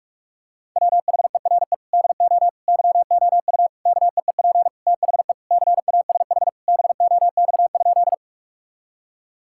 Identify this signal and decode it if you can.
Morse code, heard as a keyed tone, about 41 wpm, 700 Hz. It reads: WHERE DO YOU KEEP THE CASH BOX?
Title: Sentences from Top 1000 Words - Review Part 5 - 45wpm